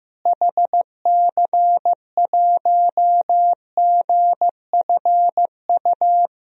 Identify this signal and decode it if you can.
Morse code, heard as a keyed tone, about 14 wpm, 700 Hz. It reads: HC1GFU
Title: DXCC-100 Call Signs - 6 Characters - Repeated 3x - 15wpm